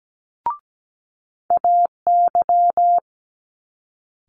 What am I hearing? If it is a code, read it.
Morse code, heard as a keyed tone, about 17 wpm, 700 Hz. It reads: AY
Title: Multi-letter Phonemes Quick Fire 17wpm